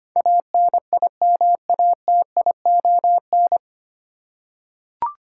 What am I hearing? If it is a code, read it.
Morse code, heard as a keyed tone, about 25 wpm, 700 Hz. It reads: ANIMATION
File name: Nine Letter Words - 25wpm